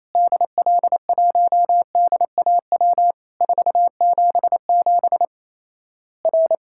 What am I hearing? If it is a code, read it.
Morse code, heard as a keyed tone, about 28 wpm, 700 Hz. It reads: DL1DAW 477 R
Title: Worked All Europe DX Contest - QTC Exchange - 28wpm